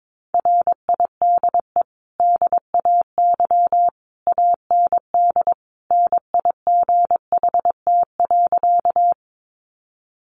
Morse code, heard as a keyed tone, about 22 wpm, 700 Hz. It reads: RIDE DAY AND NIGHT.
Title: Sentences from Top 1200 Words 22wpm